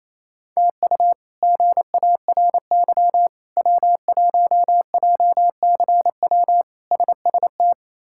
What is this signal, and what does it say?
Morse code, heard as a keyed tone, about 28 wpm, 700 Hz. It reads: TU GARY W1JCW SST
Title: K1USN SST QSO Practice - 28wpm